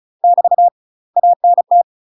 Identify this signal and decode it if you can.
Morse code, heard as a keyed tone, about 34 wpm, 700 Hz. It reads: <BT> ANT
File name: UK MASTER QSOs TRAINING / 35WPM / UK MASTER QSOs 0018